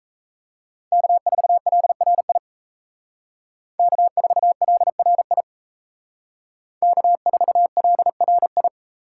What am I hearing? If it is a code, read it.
Morse code, heard as a keyed tone, about 34 wpm, 700 Hz. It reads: K4LRI K4LRI K4LRI
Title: Speed Racing - USA Call Signs 28wpm